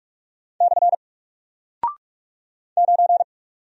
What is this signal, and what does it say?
Morse code, heard as a keyed tone, about 45 wpm, 700 Hz. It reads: / 9
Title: Single Character (Letter:Number:Punctuation:Prosign) Alt 45wpm